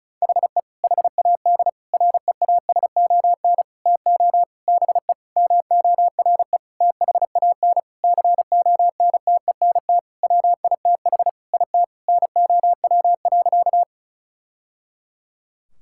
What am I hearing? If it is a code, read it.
Morse code, heard as a keyed tone, about 36 wpm, 700 Hz. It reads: HE HAD REASON TO BE MORE THAN CONTENT WITH IT NOW.
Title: Sentences from Top 1000 Words - Review Part 6 - 35wpm